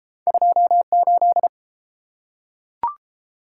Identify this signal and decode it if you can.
Morse code, heard as a keyed tone, about 30 wpm, 700 Hz. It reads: 28